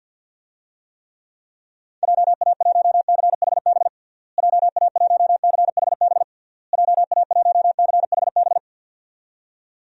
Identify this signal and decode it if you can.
Morse code, heard as a keyed tone, about 53 wpm, 700 Hz. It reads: JA1CHB JA1CHB JA1CHB
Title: DXCC-100 Call Signs - 6 Characters - Repeated 3x - 50wpm